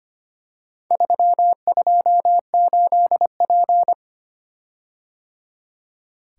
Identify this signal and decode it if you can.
Morse code, heard as a keyed tone, about 25 wpm, 700 Hz. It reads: 328P
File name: practice / 25wpm / Four Letter-Number Combinations - 25wpm